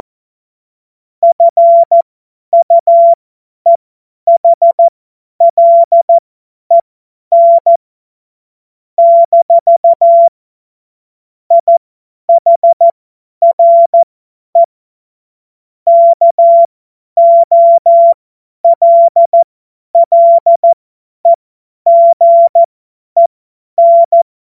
Morse code, heard as a keyed tone, about 14 wpm, spaced as for 10 wpm, 680 Hz. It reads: FUEHLEN - IHRE KOLLEGEN
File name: Morsekurs / Witze / Witz 23. Dezember 2025 50 F70